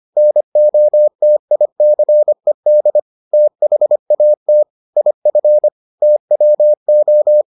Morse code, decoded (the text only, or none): NOTICED THAT IF TWO